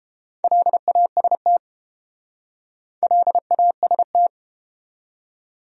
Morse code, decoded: LAST LAST

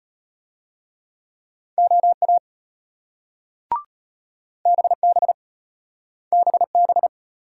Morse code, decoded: OA BB BB

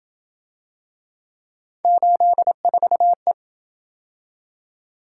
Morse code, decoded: 84E

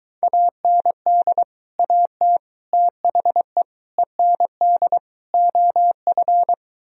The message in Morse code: AND AT THE END OF

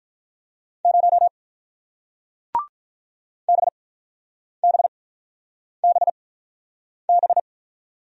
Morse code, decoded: 0 BBBB

